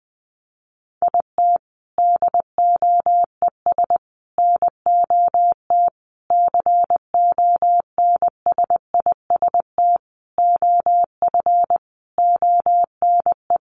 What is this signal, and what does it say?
Morse code, heard as a keyed tone, about 20 wpm, 700 Hz. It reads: IT DOES NOT CONSIST OF ONE
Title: Sentences from Top 1400 Words 20wpm